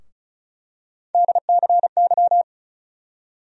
Morse code, heard as a keyed tone, about 34 wpm, 700 Hz. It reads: DCY
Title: USA Call Sign Postfixes - 3 Letters 35wpm